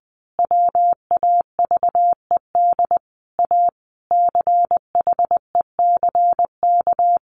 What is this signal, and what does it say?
Morse code, heard as a keyed tone, about 20 wpm, 700 Hz. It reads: WAVED A CHECK
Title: A Sound Of Thunder 20wpm